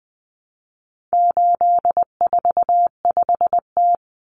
Morse code, heard as a keyed tone, about 20 wpm, 700 Hz. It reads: 845T
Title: Common Cut Numbers - 4-Digit 20wpm